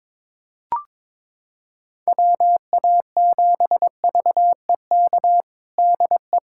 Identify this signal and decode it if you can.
Morse code, heard as a keyed tone, about 22 wpm, 700 Hz. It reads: WA7VEK DE
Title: USA Callsign DE Callsign - 6 Characters - 22wpm